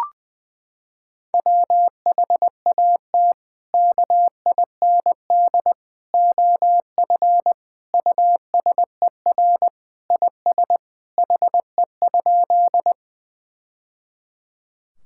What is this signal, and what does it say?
Morse code, heard as a keyed tone, about 20 wpm, 700 Hz. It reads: WHAT KIND OF USER IS HE?
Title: Sentences from Top 1000 Words - Review Part 5 - 20wpm